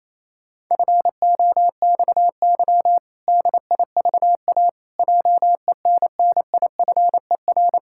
Morse code, heard as a keyed tone, about 28 wpm, 700 Hz. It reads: FOXY DIVA JENNIFER
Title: Pangram Challenge v2 - 28wpm